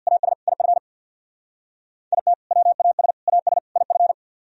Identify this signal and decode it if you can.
Morse code, heard as a keyed tone, about 60 wpm, 700 Hz. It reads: AS IF IT WAS AS IF
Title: Speed-Racing - Sets of 4 Words 50wpm